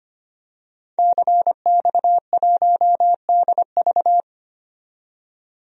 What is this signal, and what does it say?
Morse code, heard as a keyed tone, about 25 wpm, 700 Hz. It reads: CX1DV
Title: DXCC-100 Call Signs - 5 Characters - 25wpm